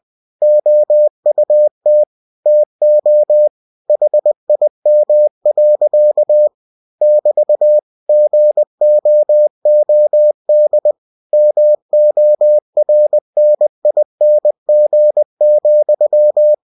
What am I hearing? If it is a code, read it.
Morse code, heard as a keyed tone, about 20 wpm, 600 Hz. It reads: OUT TO HIM. <BT> GOOD MORNING,
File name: cw-books / winnie-the-pooh / winnie-20wpm-page-0053